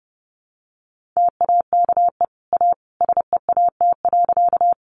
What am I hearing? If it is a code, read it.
Morse code, heard as a keyed tone, about 29 wpm, 700 Hz. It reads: TAKE A SEAT.
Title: Sentences from Top 1500 Words 30wpm